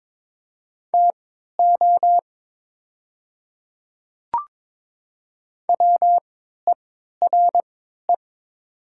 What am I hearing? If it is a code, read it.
Morse code, heard as a keyed tone, about 22 wpm, spaced as for 13 wpm, 700 Hz. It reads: TO WERE
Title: 3x Character Spacing - Top 100 Words - 22wpm